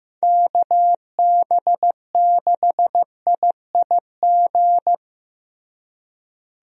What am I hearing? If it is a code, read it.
Morse code, heard as a keyed tone, about 15 wpm, 700 Hz. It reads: KB6IIG
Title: USA Call Signs - 6 Characters - Repeated 2x - 15wpm